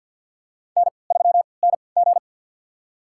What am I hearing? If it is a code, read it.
Morse code, heard as a keyed tone, about 49 wpm, spaced as for 28 wpm, 700 Hz. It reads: N3NG
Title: USA Call Signs - 4 Characters - 3x Character Spacing - 50wpm